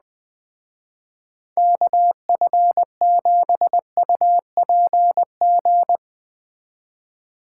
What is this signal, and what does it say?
Morse code, heard as a keyed tone, about 20 wpm, 700 Hz. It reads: KF7UPG
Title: Speed Racing - USA 2x3 Call Signs 17wpm